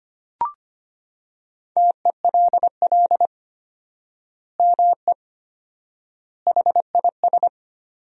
Morse code, heard as a keyed tone, about 25 wpm, 700 Hz. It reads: TELL ME HIS
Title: 4x Word-Spacing - Sentences from Top 200 Words - 25wpm